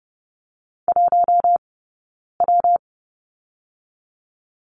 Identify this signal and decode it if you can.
Morse code, heard as a keyed tone, about 30 wpm, 700 Hz. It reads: 1 W
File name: Lesson 19d - Letter w - Two Characters 30wpm